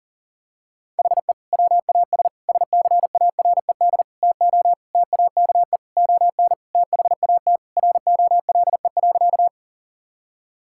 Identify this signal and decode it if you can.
Morse code, heard as a keyed tone, about 40 wpm, 700 Hz. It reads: HE WAS SCARED TO TAKE ON THAT ROLE.
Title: Sentences from Top 1000 Words - Review Part 6 - 40wpm